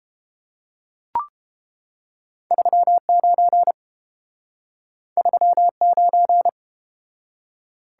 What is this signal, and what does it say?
Morse code, heard as a keyed tone, about 29 wpm, 700 Hz. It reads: 39 39